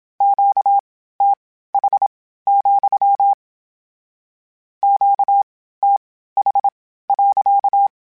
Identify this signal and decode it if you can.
Morse code, heard as a keyed tone, about 27 wpm, spaced as for 15 wpm, 800 Hz. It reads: QTH, QTH.